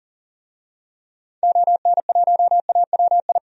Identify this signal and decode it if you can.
Morse code, heard as a keyed tone, about 40 wpm, 700 Hz. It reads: ON1AWI
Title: DXCC-100 Call Signs - 6 Characters - 40wpm